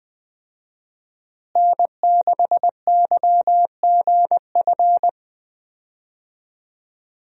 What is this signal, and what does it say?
Morse code, heard as a keyed tone, about 20 wpm, 700 Hz. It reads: N6YGF
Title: USA Call Signs - 5 Characters - Repeated 3x - 20wpm